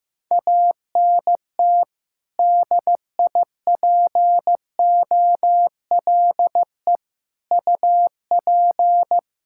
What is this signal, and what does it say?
Morse code, heard as a keyed tone, about 15 wpm, 700 Hz. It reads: ANT DIPOLE UP
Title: Practice QSO - Exchange 2 - Info - 15wpm